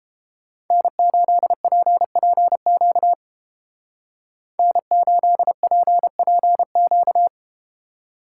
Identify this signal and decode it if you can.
Morse code, heard as a keyed tone, about 32 wpm, 700 Hz. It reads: N8PPQ N8PPQ